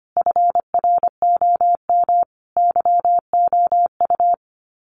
Morse code, heard as a keyed tone, about 25 wpm, 700 Hz. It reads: FROM YOU
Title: Common QSO Phrases 25wpm